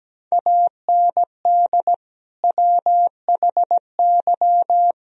Morse code, heard as a keyed tone, about 17 wpm, 700 Hz. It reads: AND WHY
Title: Sentences from Top 1000 Words - Review Part 4 - 17wpm